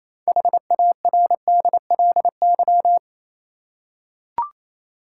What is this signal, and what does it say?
Morse code, heard as a keyed tone, about 28 wpm, 700 Hz. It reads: HARDLY
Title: Six Letter Words - 28wpm